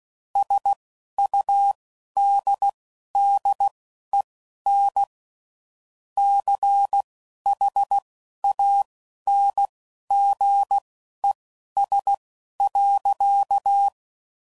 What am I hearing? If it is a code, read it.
Morse code, heard as a keyed tone, about 16 wpm, spaced as for 11 wpm, 800 Hz. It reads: SUDDEN CHANGES.
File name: CW Code Groups @10WPM / Technical Subjects / Reactance